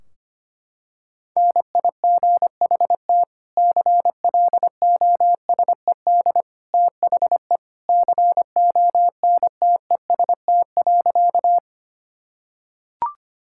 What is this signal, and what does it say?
Morse code, heard as a keyed tone, about 25 wpm, 700 Hz. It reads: NIGHT CLOSED THE CONTEST.
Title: Sentences from Top 1600 Words 25wpm